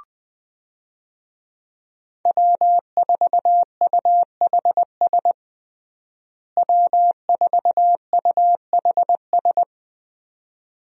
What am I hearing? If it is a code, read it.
Morse code, heard as a keyed tone, about 20 wpm, 700 Hz. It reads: W4UHS W4UHS
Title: USA Call Signs - 5 Characters - Repeated 2x - 20wpm